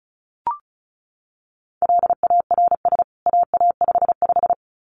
Morse code, heard as a keyed tone, about 34 wpm, 700 Hz. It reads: LARS AA55
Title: CWT Exchange 35wpm